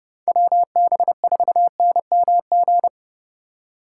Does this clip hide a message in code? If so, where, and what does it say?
Yes: Morse code from 0.3 to 2.9 s, heard as a keyed tone, about 40 wpm, 700 Hz. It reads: WB4NMG